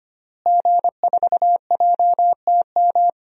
Morse code, heard as a keyed tone, about 25 wpm, 700 Hz. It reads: G4JTM
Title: DXCC-100 Call Signs - 5 Characters - Repeated 3x - 25wpm